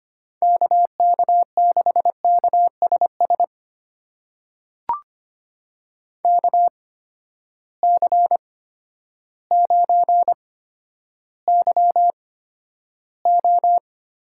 Morse code, heard as a keyed tone, about 25 wpm, spaced as for 11 wpm, 700 Hz. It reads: KK6KSS KC9YO
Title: USA Call Signs - 6 Characters - 8x Character Spacing - 25wpm